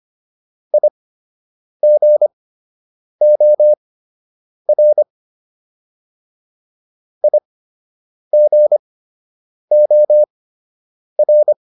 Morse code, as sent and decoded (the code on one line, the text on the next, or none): .. --. --- .-. / .. --. --- .-.
IGOR IGOR